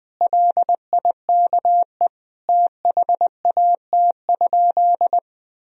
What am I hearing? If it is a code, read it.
Morse code, heard as a keyed tone, about 20 wpm, 700 Hz. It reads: LIKE THAT?
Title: Sentences from Top 1000 Words - Review Part 4 - 20wpm